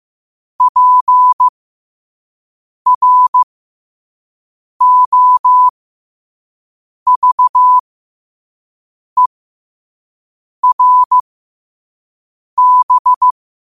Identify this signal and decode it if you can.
Morse code, heard as a keyed tone, about 15 wpm, spaced as for 5 wpm, 1000 Hz. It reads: PROVERB
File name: qotdfiles / 5wpm